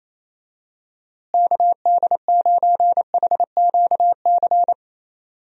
Morse code, heard as a keyed tone, about 28 wpm, 700 Hz. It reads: KD9HQC
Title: USA Call Signs - 6 Characters - Repeated 3x - 28wpm